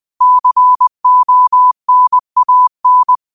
Morse code, heard as a keyed tone, about 20 wpm, 1000 Hz. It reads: CONAN
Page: New quotes every day in morse code at 20 Words per minute.